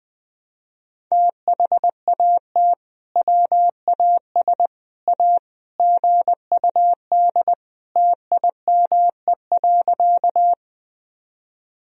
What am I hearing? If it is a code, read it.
Morse code, heard as a keyed tone, about 20 wpm, 700 Hz. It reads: THAT WAS A GUD TIME.
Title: Sentences from Top 200 Words and Abbreviations 20wpm